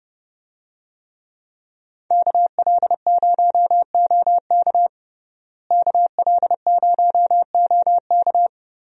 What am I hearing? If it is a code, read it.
Morse code, heard as a keyed tone, about 30 wpm, 700 Hz. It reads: KL0OK KL0OK
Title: USA Call Signs - 5 Characters - Repeated 2x - 30wpm